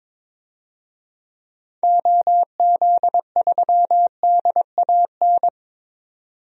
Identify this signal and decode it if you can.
Morse code, heard as a keyed tone, about 22 wpm, 700 Hz. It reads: OZ3DAN
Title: DXCC-100 Call Signs - 6 Characters - 22wpm